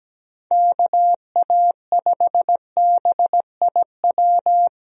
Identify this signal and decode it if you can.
Morse code, heard as a keyed tone, about 17 wpm, 700 Hz. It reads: KA5BIW
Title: USA Call Signs - 6 Characters - 17wpm